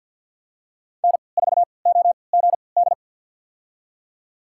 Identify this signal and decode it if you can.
Morse code, heard as a keyed tone, about 49 wpm, spaced as for 28 wpm, 700 Hz. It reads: N4OGD
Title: USA Call Signs - 5 Characters - 3x Character Spacing - 50wpm